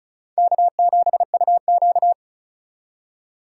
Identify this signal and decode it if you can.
Morse code, heard as a keyed tone, about 48 wpm, spaced as for 41 wpm, 700 Hz. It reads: K7UQ
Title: Speed Racing - USA Call Signs 35wpm